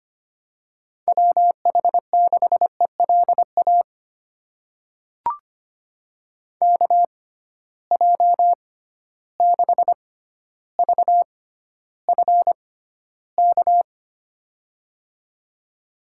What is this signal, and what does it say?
Morse code, heard as a keyed tone, about 25 wpm, spaced as for 13 wpm, 700 Hz. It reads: WH6ELA KJ6VFK
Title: USA Call Signs - 6 Characters - 6x Character Spacing - 25wpm